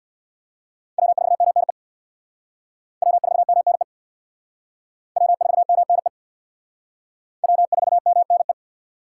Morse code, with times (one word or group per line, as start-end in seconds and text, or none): 1.0-1.7 W4MNE
3.0-3.8 W4MNE
5.2-6.1 W4MNE
7.4-8.5 W4MNE